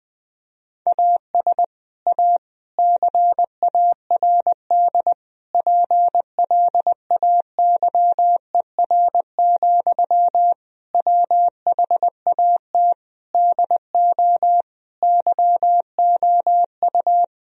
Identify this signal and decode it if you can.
Morse code, heard as a keyed tone, about 20 wpm, 700 Hz. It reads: AS A CARD PLAYER, WHAT DO YOU
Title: Sentences from Top 1000 Words - Review Part 4 - 20wpm